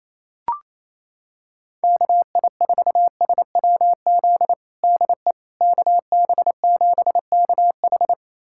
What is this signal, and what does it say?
Morse code, heard as a keyed tone, about 28 wpm, 700 Hz. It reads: KI4SWZ DE KB7KH
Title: USA Callsign DE Callsign 28wpm